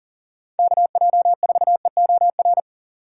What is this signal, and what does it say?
Morse code, heard as a keyed tone, about 37 wpm, 700 Hz. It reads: KJ4EOR
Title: USA Call Signs - 6 Characters - 2x Character Spacing - 40wpm